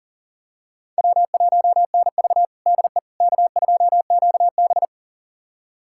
Morse code, heard as a keyed tone, about 37 wpm, 700 Hz. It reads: W1NV DE K2QB
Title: USA Callsign DE Callsign - 4 Characters - 40wpm